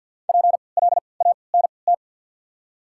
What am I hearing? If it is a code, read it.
Morse code, heard as a keyed tone, about 53 wpm, 690 Hz. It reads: P L A N T
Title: 3x Character Spacing - Top 300 Words - 50wpm